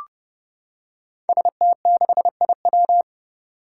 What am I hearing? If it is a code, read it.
Morse code, heard as a keyed tone, about 29 wpm, 700 Hz. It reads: ST6IW
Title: Five Challenging Letter-Number Combinations 30wpm